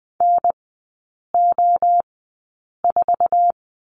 Morse code, heard as a keyed tone, about 20 wpm, spaced as for 8 wpm, 700 Hz. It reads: NO4
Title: 053 Lesson 10e - Letter r - Three Characters 20wpm